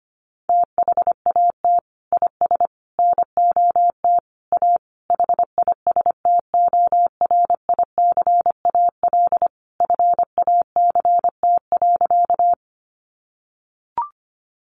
Morse code, heard as a keyed tone, about 25 wpm, 700 Hz. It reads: THAT IS NOT A HISTORICAL FACT.
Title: Sentences from Top 600 Words 25wpm